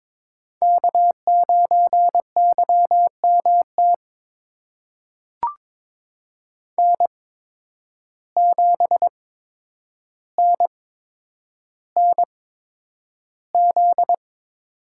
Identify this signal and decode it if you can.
Morse code, heard as a keyed tone, about 22 wpm, spaced as for 9 wpm, 700 Hz. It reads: K9YMT N7NNZ
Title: USA Call Signs - 5 Characters - 8x Character Spacing - 22wpm